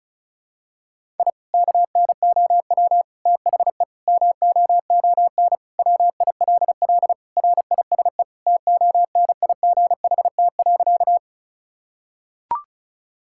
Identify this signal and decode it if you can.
Morse code, heard as a keyed tone, about 36 wpm, 700 Hz. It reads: I KNOW THE MOON WILL RISE TONIGHT.
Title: Sentences from Top 1000 Words - Review Part 6 - 35wpm